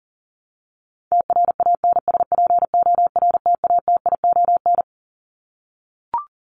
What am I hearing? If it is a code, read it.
Morse code, heard as a keyed tone, about 40 wpm, 700 Hz. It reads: TRANSPORTATION
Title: Top 1501-1750 Words 40wpm